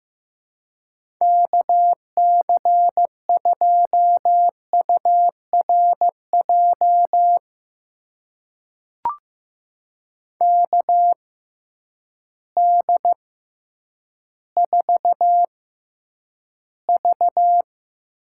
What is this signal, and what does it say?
Morse code, heard as a keyed tone, about 15 wpm, spaced as for 9 wpm, 700 Hz. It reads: KC2URJ KD4V
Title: USA Call Signs - 6 Characters - 6x Character Spacing - 15wpm